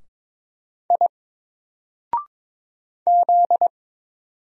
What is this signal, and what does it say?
Morse code, heard as a keyed tone, about 22 wpm, 700 Hz. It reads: I Z
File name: Single-Letter-Number-22wpm